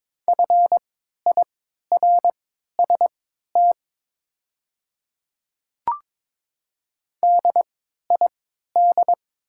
Morse code, heard as a keyed tone, about 22 wpm, spaced as for 12 wpm, 700 Hz. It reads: FIRST DID
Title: 3x Character Spacing - Top 100 Words - 22wpm